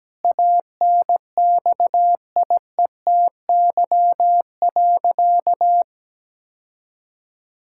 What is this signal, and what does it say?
Morse code, heard as a keyed tone, about 17 wpm, 700 Hz. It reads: ANXIETY.
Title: Sentences from Top 1600 Words 17wpm